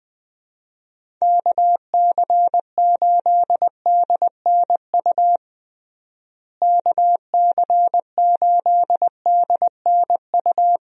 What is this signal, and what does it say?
Morse code, heard as a keyed tone, about 20 wpm, 700 Hz. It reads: KC8DNU KC8DNU
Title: USA Call Signs - 6 Characters - Repeated 3x - 20wpm